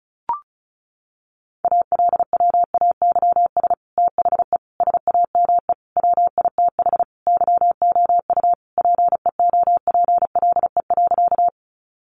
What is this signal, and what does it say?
Morse code, heard as a keyed tone, about 34 wpm, 700 Hz. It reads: ALWAYS THE SAME WITH YOU PEOPLE.
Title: Sentences from Top 300 Words 35wpm